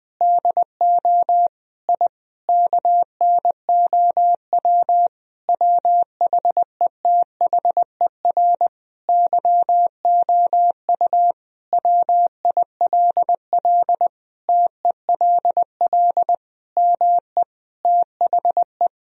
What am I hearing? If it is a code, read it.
Morse code, heard as a keyed tone, about 20 wpm, 700 Hz. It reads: DO I KNOW WHETHER YOU WILL TELL ME THE
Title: Sentences from Top 1000 Words - Review Part 3 - 20wpm